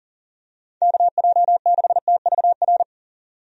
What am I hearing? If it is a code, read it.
Morse code, heard as a keyed tone, about 40 wpm, 700 Hz. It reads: KJ6TVR
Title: USA Call Signs - 6 Characters - 40wpm